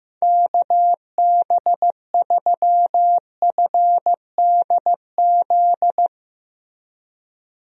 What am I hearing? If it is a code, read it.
Morse code, heard as a keyed tone, about 15 wpm, 700 Hz. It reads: KB3FDZ
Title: USA Call Signs - 6 Characters - 15wpm